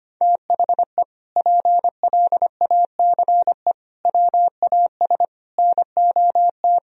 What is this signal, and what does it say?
Morse code, heard as a keyed tone, about 25 wpm, 700 Hz. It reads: THE PLACE WAS NOT
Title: Sentences from Top 1000 Words - Review Part 1 - 25wpm